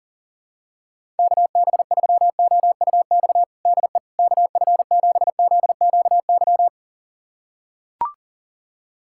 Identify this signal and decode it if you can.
Morse code, heard as a keyed tone, about 40 wpm, 700 Hz. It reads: KB3OUX DE KF7ZQY
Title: USA Callsign DE Callsign - 6 Characters - 40wpm